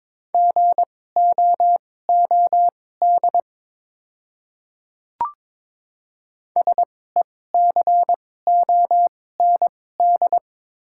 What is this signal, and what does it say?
Morse code, heard as a keyed tone, about 22 wpm, spaced as for 16 wpm, 700 Hz. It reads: GOOD SECOND
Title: 2x Character Spacing - Top 500 Words - 22wpm